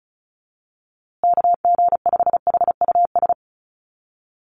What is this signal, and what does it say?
Morse code, heard as a keyed tone, about 34 wpm, 700 Hz. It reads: KG5HUS
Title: USA Call Signs Repeated 3x 35wpm